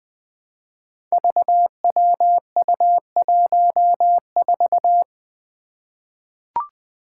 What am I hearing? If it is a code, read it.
Morse code, heard as a keyed tone, about 20 wpm, 700 Hz. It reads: VWU14
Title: Five Challenging Letter-Number Combinations 20wpm